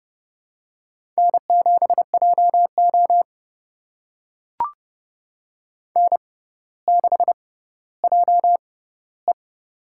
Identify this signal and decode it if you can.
Morse code, heard as a keyed tone, about 32 wpm, spaced as for 16 wpm, 700 Hz. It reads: N7JO N6JE
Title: USA Call Signs - 4 Characters - 6x Character Spacing - 30wpm